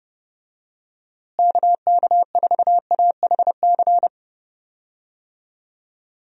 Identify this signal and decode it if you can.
Morse code, heard as a keyed tone, about 32 wpm, 700 Hz. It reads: KK4AHC
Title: USA Call Signs - 6 Characters - Repeated 3x - 30wpm